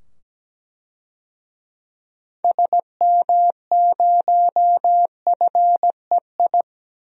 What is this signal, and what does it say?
Morse code, heard as a keyed tone, about 17 wpm, 700 Hz. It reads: SM0FEI
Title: DXCC-100 Call Signs - 6 Characters - 17wpm